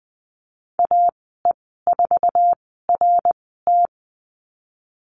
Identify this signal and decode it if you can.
Morse code, heard as a keyed tone, about 19 wpm, spaced as for 14 wpm, 700 Hz. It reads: AE4RT
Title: 055 Lesson 10g - Letter r - Call Signs 20wpm